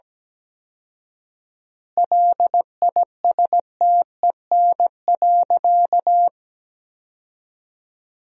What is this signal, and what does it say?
Morse code, heard as a keyed tone, about 17 wpm, 700 Hz. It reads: LISTEN.
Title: 4x Word-Spacing - Sentences from Top 500 Words - 17wpm